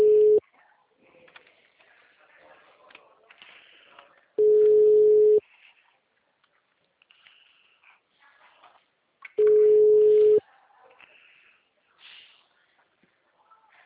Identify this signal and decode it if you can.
Ringback tone (Europe, 425 Hz).